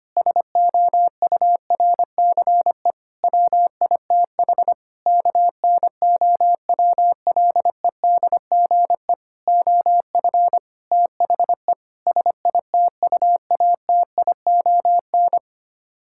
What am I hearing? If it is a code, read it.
Morse code, heard as a keyed tone, about 25 wpm, 700 Hz. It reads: SOURCE WITH KNOWLEDGE OF THE SITUATION
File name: Sets of 9 Words - 25wpm